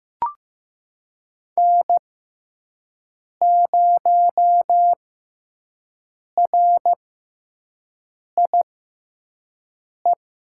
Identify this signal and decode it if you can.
Morse code, heard as a keyed tone, about 15 wpm, spaced as for 5 wpm, 700 Hz. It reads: N0RIE
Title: USA Call Signs - 5 Characters - 6x Character Spacing - 15wpm